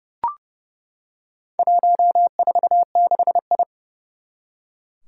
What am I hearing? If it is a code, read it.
Morse code, heard as a keyed tone, about 31 wpm, 700 Hz. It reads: 146I
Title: Four Challenging Letter-Number Combinations 30wpm